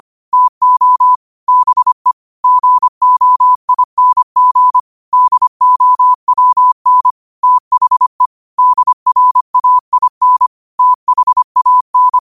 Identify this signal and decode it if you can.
Morse code, heard as a keyed tone, about 25 wpm, 1000 Hz. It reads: TO BE GOING DOWN THE DRAIN THAN
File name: qotdfiles / 25wpm